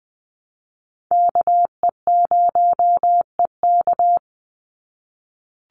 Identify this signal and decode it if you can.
Morse code, heard as a keyed tone, about 20 wpm, 700 Hz. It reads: KE0EK
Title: Call Signs - 20wpm